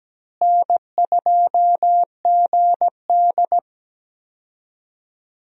Morse code, heard as a keyed tone, about 17 wpm, 700 Hz. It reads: N2GD
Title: USA Call Signs - 4 Characters - Repeated 2x - 17wpm